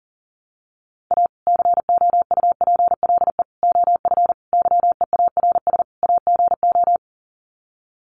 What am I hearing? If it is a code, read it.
Morse code, heard as a keyed tone, about 37 wpm, 700 Hz. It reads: A COUPLE OF YEARS AGO
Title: Sets of 5 Words 40wpm